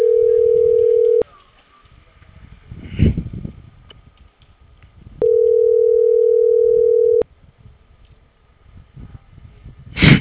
Ringback tone (North America), 440 + 480 Hz.